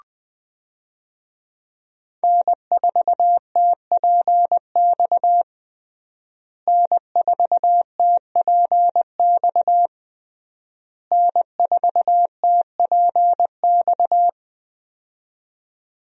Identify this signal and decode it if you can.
Morse code, heard as a keyed tone, about 21 wpm, 700 Hz. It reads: N4TPX N4TPX N4TPX
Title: USA Call Signs - 5 Characters - Repeated 3x - 20wpm